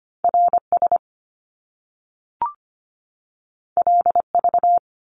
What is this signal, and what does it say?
Morse code, heard as a keyed tone, about 25 wpm, 700 Hz. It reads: RS LV
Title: Top Two Letter Combinations in Words 25wpm